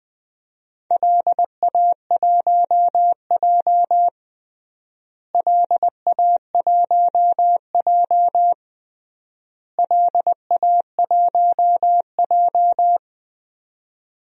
Morse code, heard as a keyed tone, about 21 wpm, 700 Hz. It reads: LA1J LA1J LA1J
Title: DXCC-100 Call Signs - 4 Characters - Repeated 3x - 20wpm